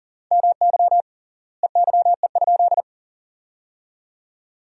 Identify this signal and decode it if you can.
Morse code, heard as a keyed tone, about 40 wpm, 700 Hz. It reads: MY EYE?
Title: 3x Word-Spacing - Sentences from Top 500 Words - 40wpm